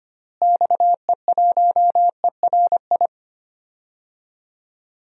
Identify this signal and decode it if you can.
Morse code, heard as a keyed tone, about 25 wpm, 700 Hz. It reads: XE1ERI
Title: DXCC-100 Call Signs - 6 Characters - Repeated 3x - 25wpm